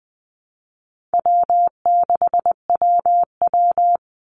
Morse code, heard as a keyed tone, about 20 wpm, 700 Hz. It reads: W6WW
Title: NA SOTA Chaser Callsigns 20wpm